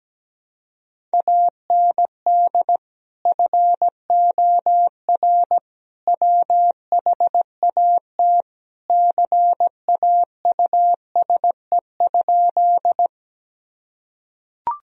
Morse code, heard as a keyed tone, about 17 wpm, 700 Hz. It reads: AND FOR WHAT CAUSE?
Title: Sentences from Top 1000 Words - Review Part 1 - 17wpm